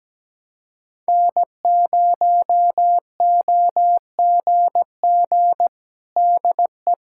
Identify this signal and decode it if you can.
Morse code, heard as a keyed tone, about 17 wpm, 700 Hz. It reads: N0OGG DE
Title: USA Callsign DE Callsign 17wpm